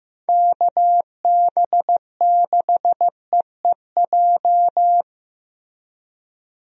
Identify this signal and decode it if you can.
Morse code, heard as a keyed tone, about 15 wpm, 700 Hz. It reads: KB6EEJ
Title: USA Call Signs - 6 Characters - Repeated 2x - 15wpm